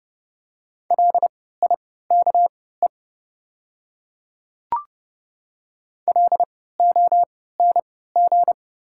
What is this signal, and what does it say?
Morse code, heard as a keyed tone, about 30 wpm, spaced as for 17 wpm, 700 Hz. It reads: LIKE LONG
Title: 3x Character Spacing - Top 100 Words - 30wpm